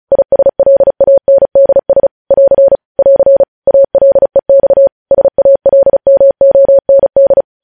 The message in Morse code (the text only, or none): ISLANDS <AR> <AR> ALEX SALMOND